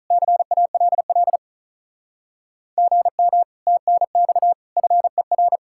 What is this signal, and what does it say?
Morse code, heard as a keyed tone, about 45 wpm, 700 Hz. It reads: CALL GM TNX FER